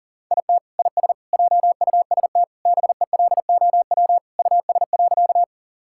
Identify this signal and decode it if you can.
Morse code, heard as a keyed tone, about 40 wpm, 700 Hz. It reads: IT IS JUST BELOW US.
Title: Sentences from Top 1000 Words - Review Part 5 - 40wpm